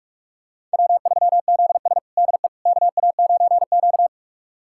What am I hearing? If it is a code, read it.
Morse code, heard as a keyed tone, about 46 wpm, 700 Hz. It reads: W3ZS DE KA9Q
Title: USA Callsign DE Callsign - 4 Characters - 45wpm